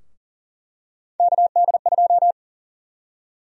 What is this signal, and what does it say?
Morse code, heard as a keyed tone, about 40 wpm, 700 Hz. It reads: KD2
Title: USA Call Sign Prefixes - 40wpm